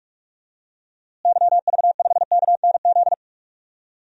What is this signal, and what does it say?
Morse code, heard as a keyed tone, about 46 wpm, 700 Hz. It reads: YV5KNZ
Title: DXCC-100 Call Signs - 6 Characters - Repeated 3x - 45wpm